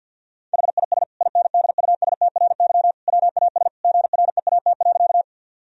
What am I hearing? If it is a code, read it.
Morse code, heard as a keyed tone, about 50 wpm, 700 Hz. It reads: HIS INDUSTRY WAS GREAT.
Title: Sentences from Top 1000 Words - Review Part 4 - 50wpm